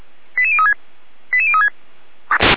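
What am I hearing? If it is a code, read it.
Five-tone selective call (ZVEI), heard as quick runs of five tones, tones of about 70 ms. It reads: 88036 88036
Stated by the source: ZVEI five-tone selective call received via SDR.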